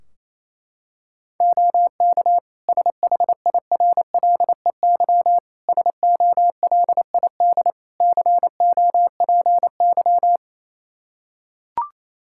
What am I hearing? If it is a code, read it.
Morse code, heard as a keyed tone, about 28 wpm, 700 Hz. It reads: OK SHIRLEY SOLID COPY
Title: Practice QSO - Exchange 3 - More Info - 28wpm